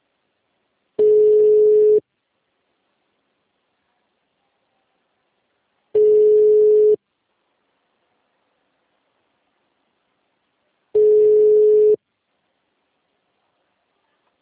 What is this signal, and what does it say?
Ringback tone (Europe, 425 Hz).
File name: out- -2012- - - .1391